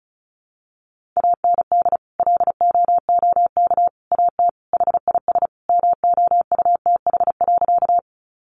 Morse code, heard as a keyed tone, about 34 wpm, 700 Hz. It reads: AND LOOK AT HIS MOUTH.
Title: Sentences from Top 1500 Words 35wpm